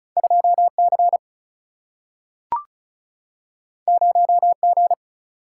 Morse code, heard as a keyed tone, about 36 wpm, 700 Hz. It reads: 2 C 0 G
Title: Single-Letter Single-Number 35wpm